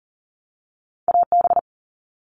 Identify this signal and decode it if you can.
Morse code, heard as a keyed tone, about 40 wpm, 700 Hz. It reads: AB
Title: Canadian Provinces and Territories - Repeated 3x - 40wpm